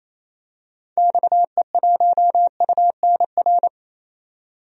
Morse code, heard as a keyed tone, about 27 wpm, 700 Hz. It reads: XE1UNR
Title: DXCC-100 Call Signs - 6 Characters - 28wpm